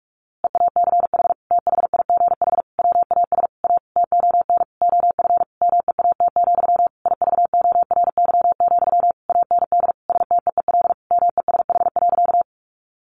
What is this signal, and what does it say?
Morse code, heard as a keyed tone, about 46 wpm, 700 Hz. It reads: EACH THIGH WAS A TON OF MEAT, IVORY, AND STEEL MESH.
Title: A Sound Of Thunder 45wpm